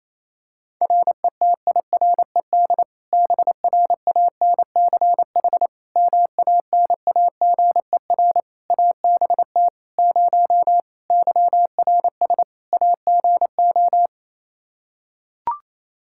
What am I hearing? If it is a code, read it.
Morse code, heard as a keyed tone, about 28 wpm, 700 Hz. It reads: RETIRED BRANCH MANAGER ABT 0 YRS AGO
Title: Practice QSO - Exchange 3 - More Info - 28wpm